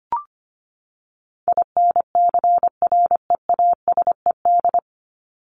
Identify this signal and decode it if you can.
Morse code, heard as a keyed tone, about 25 wpm, 700 Hz. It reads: INCREASED
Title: ED Suffix - Top 100 Words with Base Words 25wpm